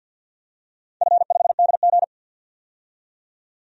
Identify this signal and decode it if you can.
Morse code, heard as a keyed tone, about 47 wpm, 700 Hz. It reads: F5DG
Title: DXCC-100 Call Signs - 4 Characters - 50wpm